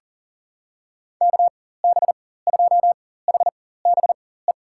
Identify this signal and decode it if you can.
Morse code, heard as a keyed tone, about 41 wpm, spaced as for 19 wpm, 700 Hz. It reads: KB2HBE
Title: USA Call Signs - 6 Characters - 4x Character Spacing - 40wpm